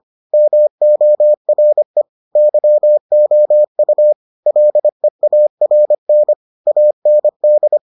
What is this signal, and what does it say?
Morse code, heard as a keyed tone, about 25 wpm, 600 Hz. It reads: MORE YOU LEARN AND